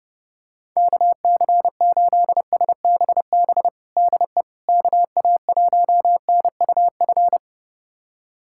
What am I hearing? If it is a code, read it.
Morse code, heard as a keyed tone, about 30 wpm, 700 Hz. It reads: KC8SBB DE KA1NUF
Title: USA Callsign DE Callsign - 6 Characters - 30wpm